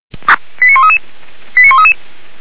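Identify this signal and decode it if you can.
Five-tone selective call (ZVEI), heard as quick runs of five tones, tones of about 70 ms. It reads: 89133 89133
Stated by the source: Selective call received by SDR, logged by Feuerwehr Orlamünde.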